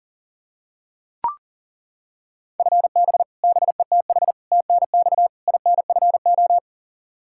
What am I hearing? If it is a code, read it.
Morse code, heard as a keyed tone, about 40 wpm, 700 Hz. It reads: FB BETH TNX INFO
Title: Practice QSO - Exchange 3 - More Info - 40wpm